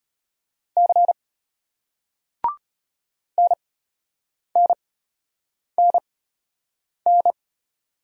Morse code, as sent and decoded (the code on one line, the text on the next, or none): -.-. / -. -. -. -.
C NNNN